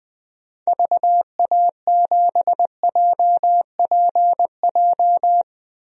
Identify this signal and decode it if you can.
Morse code, heard as a keyed tone, about 20 wpm, 700 Hz. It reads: VA7JPJ
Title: DXCC-100 Call Signs - 6 Characters - 20wpm